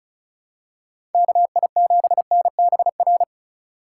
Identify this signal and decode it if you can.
Morse code, heard as a keyed tone, about 36 wpm, 700 Hz. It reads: KI7NBR
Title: USA Call Signs - 6 Characters - 4x Character Spacing - 35wpm